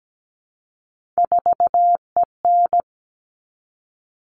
Morse code, heard as a keyed tone, about 17 wpm, 700 Hz. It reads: 4EN
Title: All Cut Numbers - 3-Digit 17wpm